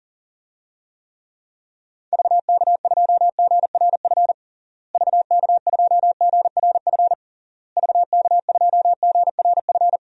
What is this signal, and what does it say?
Morse code, heard as a keyed tone, about 37 wpm, 700 Hz. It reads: VK2GRF VK2GRF VK2GRF
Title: DXCC-100 Call Signs - 6 Characters - Repeated 3x - 40wpm